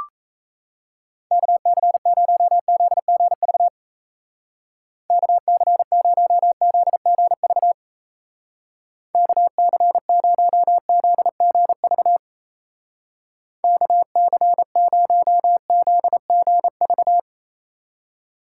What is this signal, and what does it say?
Morse code, heard as a keyed tone, about 35 wpm, 700 Hz. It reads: KC0ZGV KC0ZGV KC0ZGV KC0ZGV